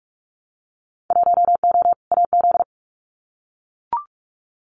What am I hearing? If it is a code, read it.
Morse code, heard as a keyed tone, about 44 wpm, 700 Hz. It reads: 1O AZ
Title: Winter Field Day Exchanges 45wpm